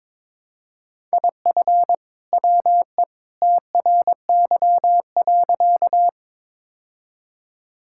Morse code, heard as a keyed tone, about 22 wpm, 700 Hz. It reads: IF WE TRY.
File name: practice / 22wpm / Sentences with 2-Word Phrases - 22wpm